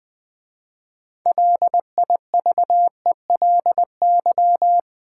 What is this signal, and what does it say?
Morse code, heard as a keyed tone, about 20 wpm, 700 Hz. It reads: LIVELY
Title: 6 Letter Words - First 3 Letters Given - 20wpm